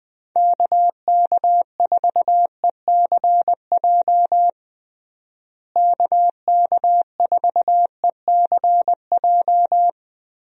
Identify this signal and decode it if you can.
Morse code, heard as a keyed tone, about 20 wpm, 700 Hz. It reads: KK4ECJ KK4ECJ
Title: USA Call Signs - 6 Characters - Repeated 3x - 20wpm